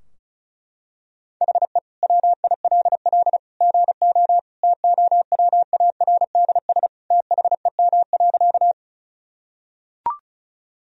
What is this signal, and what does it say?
Morse code, heard as a keyed tone, about 36 wpm, 700 Hz. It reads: HE WILL GO TOWARDS THEM.
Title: Sentences from Top 1000 Words - Review Part 2 - 35wpm